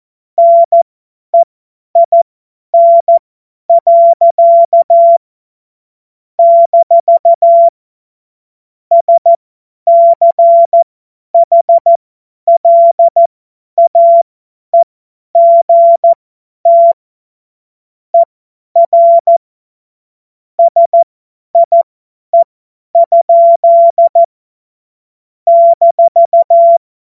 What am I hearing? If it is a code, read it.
Morse code, heard as a keyed tone, about 14 wpm, spaced as for 10 wpm, 680 Hz. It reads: NEIN. - SCHLAEGT ER SIE? -
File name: Morsekurs / Witze / Witz 01. November 2022 50 F70